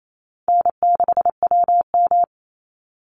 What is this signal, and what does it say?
Morse code, heard as a keyed tone, about 28 wpm, 700 Hz. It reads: N6WM
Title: Easy Call Signs Repeated 2x - 28wpm